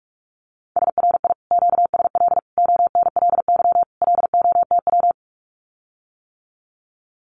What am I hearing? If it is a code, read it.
Morse code, heard as a keyed tone, about 46 wpm, 700 Hz. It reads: SRI QSL ONLY LOTW
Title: Common QSO Phrases Encore 45wpm